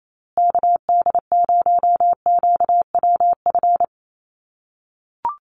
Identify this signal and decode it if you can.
Morse code, heard as a keyed tone, about 28 wpm, 700 Hz. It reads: KD0QWF
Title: USA Call Signs Repeated 3x 28wpm